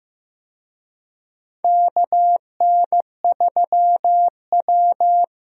Morse code, heard as a keyed tone, about 15 wpm, 700 Hz. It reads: KN3W
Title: USA Call Signs - 4 Characters - Repeated 2x - 15wpm